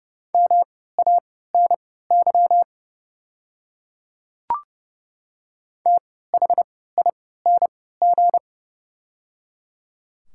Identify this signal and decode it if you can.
Morse code, heard as a keyed tone, about 30 wpm, spaced as for 17 wpm, 700 Hz. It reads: MANY THING
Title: 3x Character Spacing - Top 200 Words - 30wpm